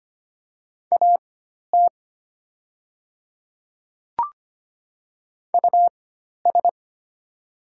Morse code, heard as a keyed tone, about 25 wpm, spaced as for 12 wpm, 690 Hz. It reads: AT US
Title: 4x Character Spacing - Top 300 Words - 25wpm